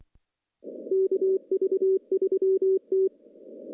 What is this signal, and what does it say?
Morse code, heard as a keyed tone, about 23 wpm, 380 Hz. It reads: KV3T
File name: CW-SS-k5tr